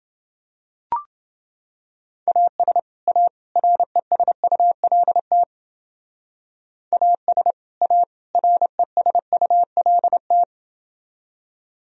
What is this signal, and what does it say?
Morse code, heard as a keyed tone, about 28 wpm, 700 Hz. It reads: AS A RESULT AS A RESULT